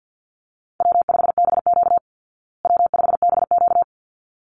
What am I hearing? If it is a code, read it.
Morse code, heard as a keyed tone, about 50 wpm, 700 Hz. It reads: W5BQ W5BQ
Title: Call Signs Repeated 3x - 50wpm